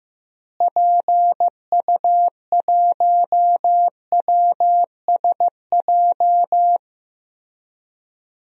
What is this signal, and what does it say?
Morse code, heard as a keyed tone, about 15 wpm, 700 Hz. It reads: PU1WSJ
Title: DXCC-100 Call Signs - 6 Characters - Repeated 3x - 15wpm